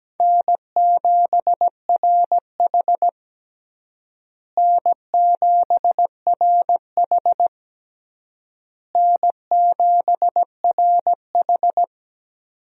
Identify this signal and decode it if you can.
Morse code, heard as a keyed tone, about 17 wpm, 700 Hz. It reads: N7RH N7RH N7RH
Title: USA Call Signs - 4 Characters - Repeated 3x - 17wpm